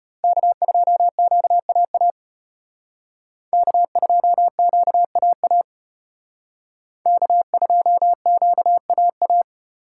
Morse code, heard as a keyed tone, about 32 wpm, 700 Hz. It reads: K2QAA K2QAA K2QAA